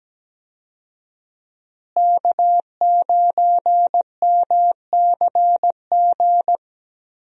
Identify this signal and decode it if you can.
Morse code, heard as a keyed tone, about 17 wpm, 700 Hz. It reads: K9MCG
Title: USA Call Signs - 5 Characters - Repeated 2x - 17wpm